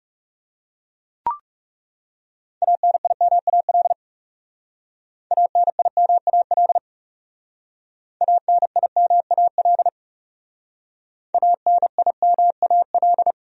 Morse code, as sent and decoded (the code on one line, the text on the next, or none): .- -. .. -- .- .-.. / .- -. .. -- .- .-.. / .- -. .. -- .- .-.. / .- -. .. -- .- .-..
ANIMAL ANIMAL ANIMAL ANIMAL